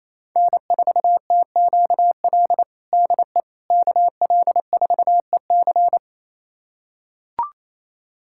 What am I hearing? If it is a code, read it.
Morse code, heard as a keyed tone, about 28 wpm, 690 Hz. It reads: N4TQL DE KL4EC
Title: USA Callsign DE Callsign - 5 Characters - 28wpm